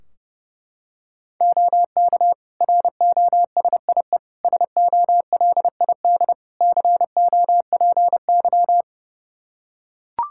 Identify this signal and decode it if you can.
Morse code, heard as a keyed tone, about 29 wpm, 700 Hz. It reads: OK ROSIE SOLID COPY
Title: Practice QSO - Exchange 3 - More Info - 30wpm